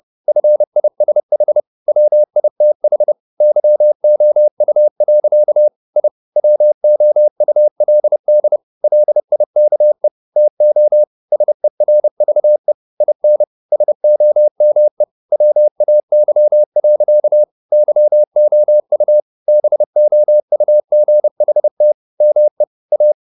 Morse code, heard as a keyed tone, about 30 wpm, 600 Hz. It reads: FISH WITH YOU. I WOULD LIKE TO SERVE IN SOME WAY. YOU BOUGHT ME A
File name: the old man and the sea / the old man and the sea 1 30wpm